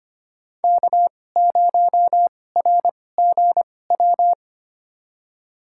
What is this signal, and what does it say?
Morse code, heard as a keyed tone, about 25 wpm, spaced as for 18 wpm, 700 Hz. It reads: K0RGW
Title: USA Call Signs - 5 Characters - 2x Character Spacing - 25wpm